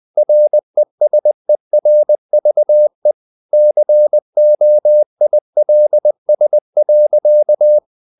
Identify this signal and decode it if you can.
Morse code, heard as a keyed tone, about 20 wpm, 600 Hz. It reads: RESERVE COILS.
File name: the old man and the sea / the old man and the sea part 1 20wpm